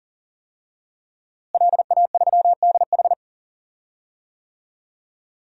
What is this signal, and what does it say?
Morse code, heard as a keyed tone, about 40 wpm, 700 Hz. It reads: LA3DH
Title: DXCC-100 Call Signs - 5 Characters - Repeated 2x - 40wpm